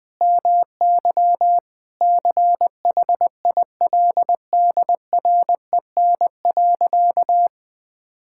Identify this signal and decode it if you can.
Morse code, heard as a keyed tone, about 21 wpm, 700 Hz. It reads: MY CHILDREN.
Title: Sentences from Top 1000 Words - Review Part 5 - 20wpm